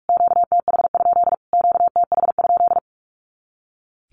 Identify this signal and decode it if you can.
Morse code, heard as a keyed tone, about 44 wpm, 700 Hz. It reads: QTH? QTH?
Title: Q-Codes Repeated 3x 45wpm